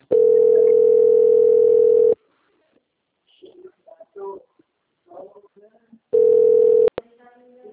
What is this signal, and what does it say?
Ringback tone (North America), 440 + 480 Hz, with speech.
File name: out- - - - - .17599